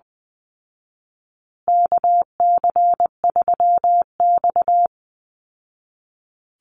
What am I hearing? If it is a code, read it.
Morse code, heard as a keyed tone, about 20 wpm, 700 Hz. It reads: KC3X
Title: Call Signs Repeated 3x - 20wpm